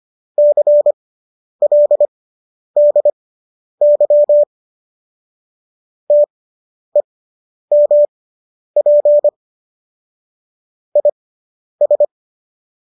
Morse code, heard as a keyed tone, about 25 wpm, spaced as for 10 wpm, 590 Hz. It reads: CLDY TEMP IS